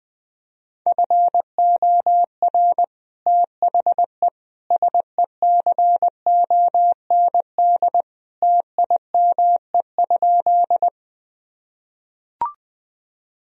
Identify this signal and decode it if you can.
Morse code, heard as a keyed tone, about 20 wpm, 700 Hz. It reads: FOR THE SECOND TIME?
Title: Sentences from Top 1000 Words - Review Part 1 - 20wpm